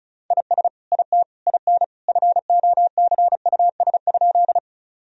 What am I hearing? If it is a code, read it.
Morse code, heard as a keyed tone, about 36 wpm, 700 Hz. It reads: IS IT IN FOCUS?
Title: Sentences from Top 1000 Words - Review Part 4 - 35wpm